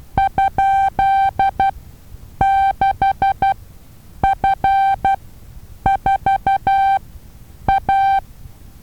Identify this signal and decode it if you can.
Morse code, heard as a keyed tone, about 12 wpm, spaced as for 8 wpm, 780 Hz. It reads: ?6F4A
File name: Taviro HA6VA B